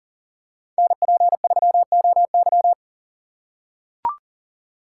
Morse code, heard as a keyed tone, about 40 wpm, 700 Hz. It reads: NP3OY
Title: USA Call Signs - 6x Character Spacing - 40wpm